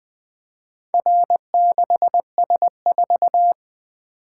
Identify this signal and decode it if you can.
Morse code, heard as a keyed tone, about 20 wpm, 700 Hz. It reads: R6S4
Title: Four Letter-Number Combinations - 20wpm